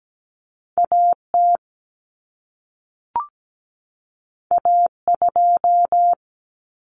Morse code, heard as a keyed tone, about 17 wpm, 700 Hz. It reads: AT A2
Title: Common Cut Numbers - 2-Digit 17wpm